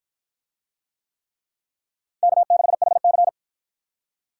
Morse code, heard as a keyed tone, about 53 wpm, 700 Hz. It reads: K6HC